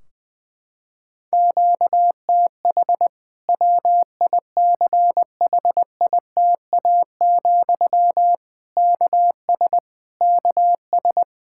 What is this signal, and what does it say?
Morse code, heard as a keyed tone, about 20 wpm, 700 Hz. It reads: QTH WICHITA, KS KS
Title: K1USN SST QSO Practice 20wpm